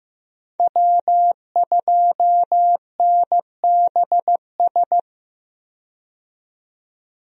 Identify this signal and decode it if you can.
Morse code, heard as a keyed tone, about 15 wpm, 700 Hz. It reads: W2NBS
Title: USA Call Signs - 5 Characters - 15wpm